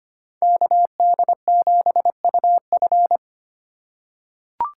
Morse code, heard as a keyed tone, about 25 wpm, 700 Hz. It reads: KD7UF
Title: USA Call Signs - 2x Character Spacing - 25wpm